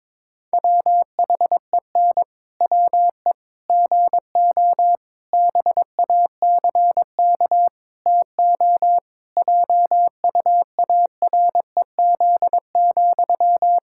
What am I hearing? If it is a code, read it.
Morse code, heard as a keyed tone, about 22 wpm, 700 Hz. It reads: WHEN WE GO BACK TO JUAREZ,
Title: Pangram Challenge v2 - 22wpm